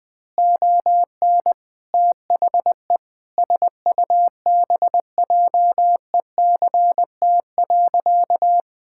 Morse code, heard as a keyed tone, about 21 wpm, 700 Hz. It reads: ON THE SUBJECT.
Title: Sentences from Top 1000 Words - Review Part 6 - 20wpm